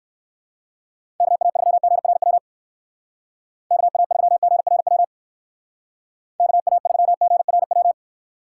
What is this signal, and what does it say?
Morse code, heard as a keyed tone, about 56 wpm, 700 Hz. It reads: KA3GRW KA3GRW KA3GRW
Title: Speed Racing - USA 2x3 Call Signs 45wpm